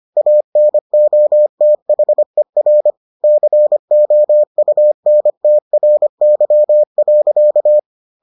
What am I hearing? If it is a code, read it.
Morse code, heard as a keyed tone, about 25 wpm, 600 Hz. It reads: ANOTHER COUNTRY.